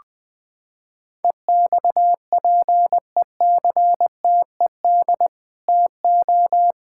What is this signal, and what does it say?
Morse code, heard as a keyed tone, about 20 wpm, 700 Hz. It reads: EXPECTED TO
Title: Sets of 8 Words - 20wpm